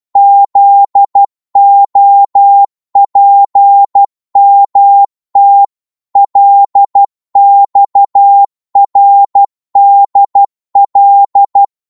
Morse code, heard as a keyed tone, about 12 wpm, 800 Hz. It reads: ZOPMT LXRDL
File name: cwmp3 / ltr-12wpm